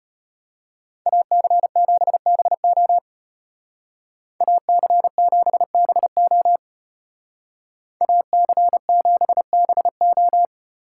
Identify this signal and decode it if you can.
Morse code, heard as a keyed tone, about 32 wpm, 700 Hz. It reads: AC7BO AC7BO AC7BO